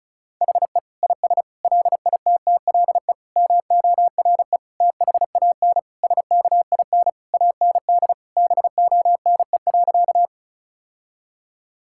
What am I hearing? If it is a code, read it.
Morse code, heard as a keyed tone, about 36 wpm, 700 Hz. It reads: HE IS LITTLE MORE THAN SKIN AND BONE.
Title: Sentences from Top 1600 Words 35wpm